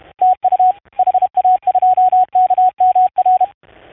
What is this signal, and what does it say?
Morse code, heard as a keyed tone, about 31 wpm, 720 Hz, with radio noise. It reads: TU HA2KMR
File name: 004-CW-Ctest-7006khz 16-27-16